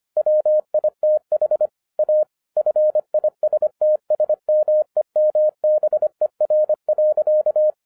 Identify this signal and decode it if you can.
Morse code, heard as a keyed tone, about 25 wpm, 610 Hz. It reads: WITH A FISTS MEMBER.
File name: FIRST FISTS CONTACT-25wpm